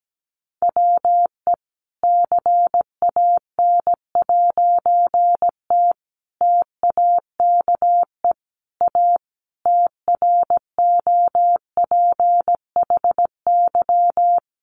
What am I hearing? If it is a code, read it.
Morse code, heard as a keyed tone, about 17 wpm, 700 Hz. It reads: WE CAN'T TAKE A TROPHY
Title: A Sound Of Thunder 17wpm